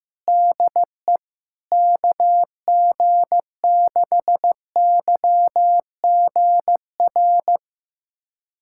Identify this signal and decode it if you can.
Morse code, heard as a keyed tone, about 15 wpm, 700 Hz. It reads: DE KG6YGR
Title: USA Callsign DE Callsign 15wpm